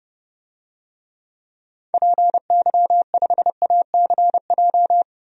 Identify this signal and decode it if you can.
Morse code, heard as a keyed tone, about 31 wpm, 700 Hz. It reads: PY5ACJ
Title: DXCC-100 Call Signs - 6 Characters - 30wpm